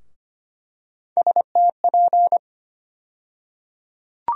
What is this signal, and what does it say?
Morse code, heard as a keyed tone, about 25 wpm, 700 Hz. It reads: STP
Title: USA Call Sign Postfixes - 3 Letters 25wpm